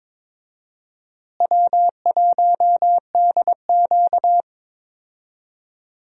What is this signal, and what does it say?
Morse code, heard as a keyed tone, about 22 wpm, 700 Hz. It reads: W1DQ
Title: USA Call Signs - 4 Characters - Repeated 2x - 22wpm